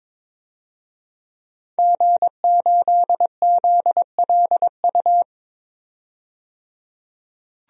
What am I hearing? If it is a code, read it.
Morse code, heard as a keyed tone, about 22 wpm, 700 Hz. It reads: G8ZLU
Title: DXCC-100 Call Signs - 5 Characters - 22wpm